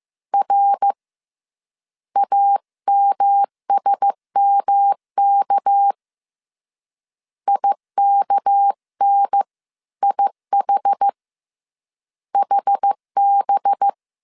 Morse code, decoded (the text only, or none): R AMSMK IKN IH HB